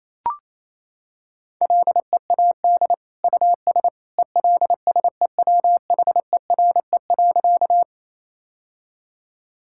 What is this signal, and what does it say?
Morse code, heard as a keyed tone, about 27 wpm, 700 Hz. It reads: LEAD US ELSEWHERE.
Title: Sentences from Top 1600 Words 28wpm